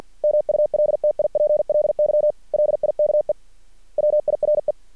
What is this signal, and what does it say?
Morse code, heard as a keyed tone, about 50 wpm, 600 Hz. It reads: MULTIPLY LIKE WIRE
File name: archived / CRSnet